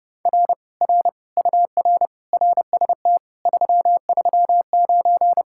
Morse code, heard as a keyed tone, about 33 wpm, 700 Hz. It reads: R R UR RST 339